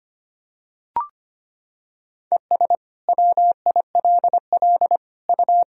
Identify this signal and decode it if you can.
Morse code, heard as a keyed tone, about 25 wpm, 700 Hz. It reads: ES WILL U
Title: Sentences from Top 400 Words and Abbreviations 25wpm